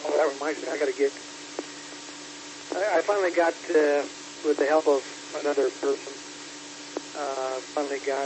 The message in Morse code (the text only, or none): none